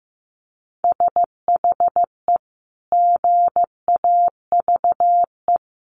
Morse code, heard as a keyed tone, about 15 wpm, 700 Hz. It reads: SHE GAVE
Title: Sentences from Top 1400 Words 15wpm